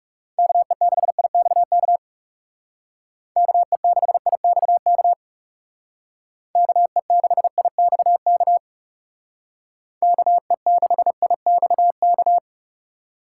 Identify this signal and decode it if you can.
Morse code, heard as a keyed tone, about 40 wpm, 700 Hz. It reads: KE6IXK KE6IXK KE6IXK KE6IXK